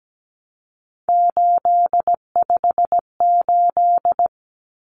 Morse code, heard as a keyed tone, about 17 wpm, 700 Hz. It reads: 858
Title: Numbers - 3-Digit 17wpm